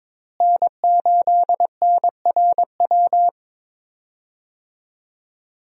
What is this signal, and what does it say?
Morse code, heard as a keyed tone, about 22 wpm, 700 Hz. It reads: N8NRW
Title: USA Call Signs - 5 Characters - Repeated 2x - 22wpm